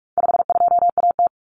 Morse code, heard as a keyed tone, about 44 wpm, 700 Hz. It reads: 52AT
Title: Common Cut Numbers - 4-Digit 45wpm